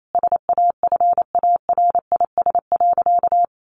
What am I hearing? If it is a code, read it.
Morse code, heard as a keyed tone, about 28 wpm, 700 Hz. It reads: SAFARIS.
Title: A Sound Of Thunder 28wpm